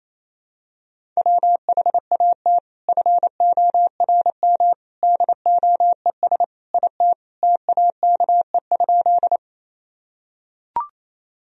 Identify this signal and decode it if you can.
Morse code, heard as a keyed tone, about 28 wpm, 700 Hz. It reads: WHAT FORM DOES IT TAKE?
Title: Sentences from Top 1000 Words - Review Part 1 - 28wpm